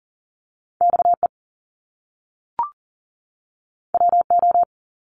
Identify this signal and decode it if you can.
Morse code, heard as a keyed tone, about 42 wpm, 700 Hz. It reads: XE WO
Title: Top Two Letter Combinations in Words 40wpm